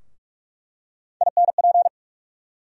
Morse code, heard as a keyed tone, about 46 wpm, 700 Hz. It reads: INP
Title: USA Call Sign Postfixes - 3 Letters 45wpm